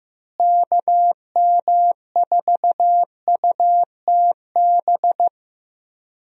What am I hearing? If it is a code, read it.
Morse code, heard as a keyed tone, about 16 wpm, 700 Hz. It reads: KM4UTB
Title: USA Call Signs - 6 Characters - Repeated 2x - 15wpm